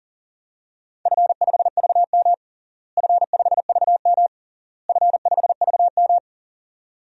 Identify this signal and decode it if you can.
Morse code, heard as a keyed tone, about 37 wpm, 700 Hz. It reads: F5VM F5VM F5VM
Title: DXCC-100 Call Signs - 4 Characters - Repeated 3x - 40wpm